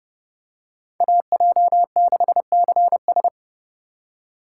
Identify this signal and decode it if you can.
Morse code, heard as a keyed tone, about 29 wpm, 700 Hz. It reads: AJ6CS